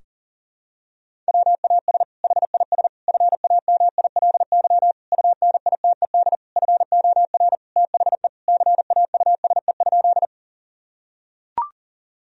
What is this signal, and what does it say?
Morse code, heard as a keyed tone, about 40 wpm, 700 Hz. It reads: WAS HIS FAMILY UNITED FOR THE CAUSE?
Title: Sentences from Top 1000 Words - Review Part 4 - 40wpm